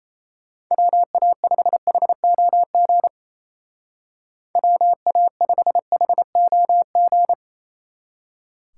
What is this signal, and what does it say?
Morse code, heard as a keyed tone, about 35 wpm, 700 Hz. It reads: WA5HOG WA5HOG
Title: Speed Racing - USA 2x3 Call Signs 28wpm